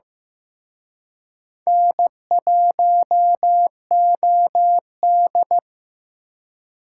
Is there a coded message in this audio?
Yes: Morse code from 1.7 to 5.6 s, heard as a keyed tone, about 19 wpm, 700 Hz. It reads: N1OD